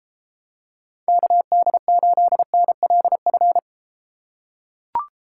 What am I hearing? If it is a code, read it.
Morse code, heard as a keyed tone, about 32 wpm, 700 Hz. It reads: KD8NLF